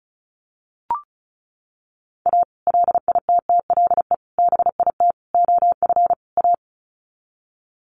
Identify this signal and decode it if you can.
Morse code, heard as a keyed tone, about 34 wpm, 700 Hz. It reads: A LITTLE BIT OF A
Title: Sets of 5 Words 35wpm